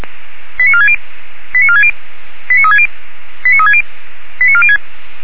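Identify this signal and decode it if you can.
Five-tone selective call (ZVEI), heard as quick runs of five tones, tones of about 70 ms. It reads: 89460 89460 89360 89360 89406
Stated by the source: ZVEI five-tone selective call received on SDR, 464 MHz.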